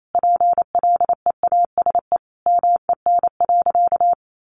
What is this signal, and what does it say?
Morse code, heard as a keyed tone, about 28 wpm, 700 Hz. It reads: PLEASE MEN.
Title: Sentences from Top 1500 Words 28wpm